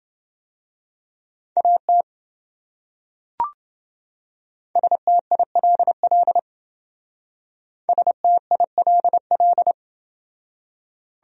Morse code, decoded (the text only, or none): AT STILL STILL